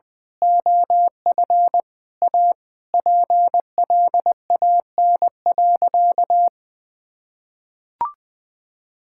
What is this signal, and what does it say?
Morse code, heard as a keyed tone, about 20 wpm, 700 Hz. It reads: OF A PLAN.
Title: Sentences with 2-Word Phrases - 20wpm